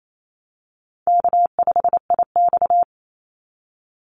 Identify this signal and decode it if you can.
Morse code, heard as a keyed tone, about 28 wpm, 700 Hz. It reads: K5IX
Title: Call Signs - 28wpm